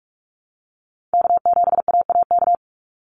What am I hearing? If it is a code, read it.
Morse code, heard as a keyed tone, about 46 wpm, 700 Hz. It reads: K7AAK
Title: USA Call Signs 45wpm